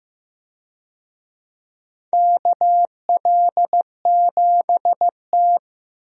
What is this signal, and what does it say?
Morse code, heard as a keyed tone, about 15 wpm, 700 Hz. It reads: KL7T
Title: USA Call Signs - 4 Characters - Repeated 2x - 15wpm